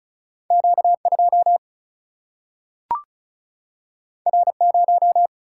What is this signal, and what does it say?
Morse code, heard as a keyed tone, about 36 wpm, 700 Hz. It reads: Q2 R0
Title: Single-Letter Single-Number 35wpm